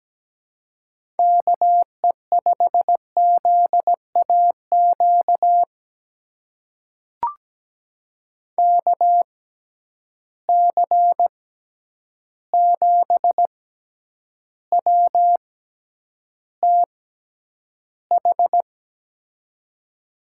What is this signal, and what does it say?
Morse code, heard as a keyed tone, about 17 wpm, spaced as for 9 wpm, 700 Hz. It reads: KE5ZAQ KC7WTH
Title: USA Call Signs - 6 Characters - 6x Character Spacing - 17wpm